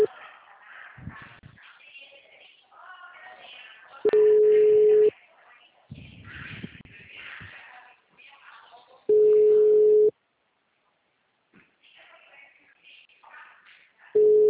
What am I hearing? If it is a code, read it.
Ringback tone (Europe, 425 Hz).
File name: out- -2050- - - .16079